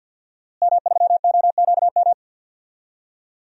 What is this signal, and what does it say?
Morse code, heard as a keyed tone, about 50 wpm, 700 Hz. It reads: M3OQM
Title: DXCC-100 Call Signs - 5 Characters - Repeated 3x - 50wpm